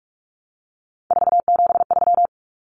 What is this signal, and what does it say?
Morse code, heard as a keyed tone, about 44 wpm, 700 Hz. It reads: 473
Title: Numbers - 3-Digit 45wpm